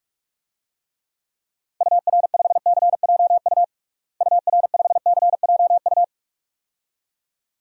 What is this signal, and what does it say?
Morse code, heard as a keyed tone, about 46 wpm, 700 Hz. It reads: UR5CJU UR5CJU
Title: DXCC-100 Call Signs - 6 Characters - Repeated 2x - 45wpm